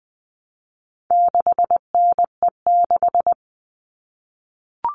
Morse code, heard as a keyed tone, about 20 wpm, 700 Hz. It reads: 6NE6
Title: All Cut Numbers - 4-Digit 20wpm